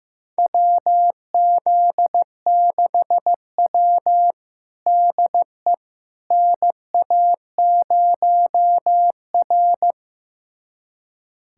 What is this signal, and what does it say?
Morse code, heard as a keyed tone, about 16 wpm, 700 Hz. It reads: WZ6W DE NA0R
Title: USA Callsign DE Callsign - 4 Characters - 15wpm